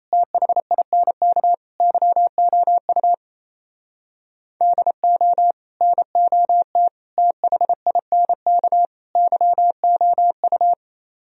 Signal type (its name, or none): Morse code